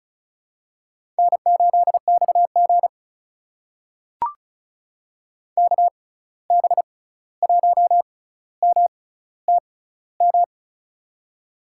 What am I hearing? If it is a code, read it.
Morse code, heard as a keyed tone, about 36 wpm, spaced as for 16 wpm, 700 Hz. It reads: N8XG KB1MTM
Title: USA Call Signs - 6x Character Spacing - 35wpm